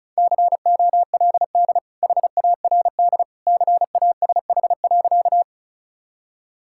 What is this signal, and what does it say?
Morse code, heard as a keyed tone, about 34 wpm, 700 Hz. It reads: COLD HARD CASH.
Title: Sentences from Top 1000 Words - Review Part 5 - 35wpm